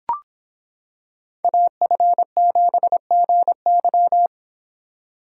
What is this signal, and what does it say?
Morse code, heard as a keyed tone, about 25 wpm, 700 Hz. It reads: AF7GY